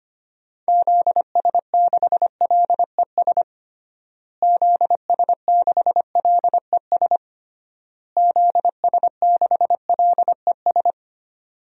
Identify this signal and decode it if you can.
Morse code, heard as a keyed tone, about 25 wpm, 700 Hz. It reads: ZS6LES ZS6LES ZS6LES
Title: DXCC-100 Call Signs - 6 Characters - Repeated 3x - 25wpm